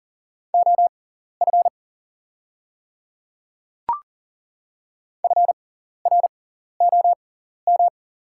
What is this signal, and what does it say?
Morse code, heard as a keyed tone, about 40 wpm, spaced as for 14 wpm, 700 Hz. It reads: OF FROM
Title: 6x Character Spacing - Top 100 Words - 40wpm